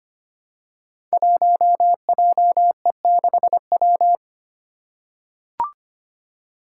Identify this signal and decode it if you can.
Morse code, heard as a keyed tone, about 25 wpm, 700 Hz. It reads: 1JE6W
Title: Five Challenging Letter-Number Combinations 25wpm